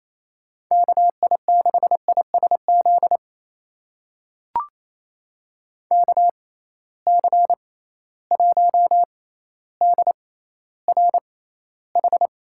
Morse code, heard as a keyed tone, about 28 wpm, spaced as for 14 wpm, 700 Hz. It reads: KI6ISZ KC1DRH
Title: USA Call Signs - 6 Characters - 6x Character Spacing - 28wpm